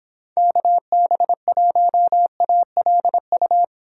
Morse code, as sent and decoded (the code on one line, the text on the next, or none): -.- -... .---- .- .-.. ..-
KB1ALU